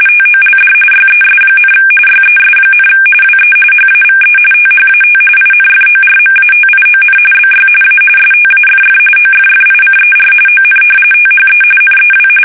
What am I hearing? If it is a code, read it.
RTTY, about 45.45 baud. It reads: TOLD YES, YES, WE HAVE THAT INFORMATION BY A STAFF OFFICER ON THE NEW
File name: fsk850 / battleshipnj